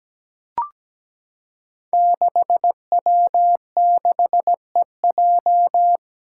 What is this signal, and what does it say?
Morse code, heard as a keyed tone, about 17 wpm, 700 Hz. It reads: 6W6EJ
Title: Five Challenging Letter-Number Combinations 17wpm